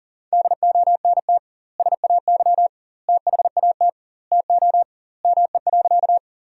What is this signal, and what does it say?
Morse code, heard as a keyed tone, about 40 wpm, 700 Hz. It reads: DONT SAY THAT TO ME.
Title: 2x Word-Spacing - Sentences from Top 300 Words - 40wpm